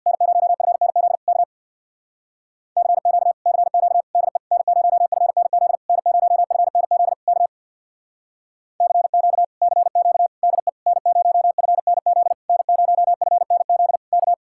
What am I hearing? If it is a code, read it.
Morse code, heard as a keyed tone, about 66 wpm, 700 Hz. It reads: N0FNZ K CQ CQ DE N0FNZ N0FNZ K CQ CQ DE N0FNZ N0FNZ K